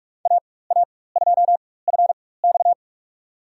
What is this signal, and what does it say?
Morse code, heard as a keyed tone, about 46 wpm, spaced as for 21 wpm, 700 Hz. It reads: AA2FX
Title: USA Call Signs - 5 Characters - 4x Character Spacing - 45wpm